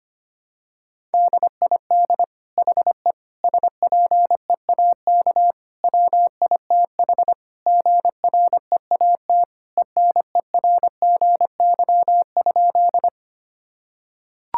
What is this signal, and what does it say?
Morse code, heard as a keyed tone, about 25 wpm, 700 Hz. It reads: DID HE SPEAK WITH GREAT ENERGY?
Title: Sentences from Top 1000 Words - Review Part 6 - 25wpm